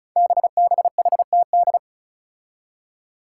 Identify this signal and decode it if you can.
Morse code, heard as a keyed tone, about 36 wpm, 700 Hz. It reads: BBHTD
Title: Five Challenging Letter-Number Combinations 35wpm